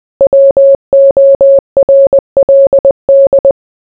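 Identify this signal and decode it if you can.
Morse code, heard as a keyed tone, about 20 wpm, 550 Hz. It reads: WORLD
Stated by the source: Morse code mixed with noise and music as background sound for sleep.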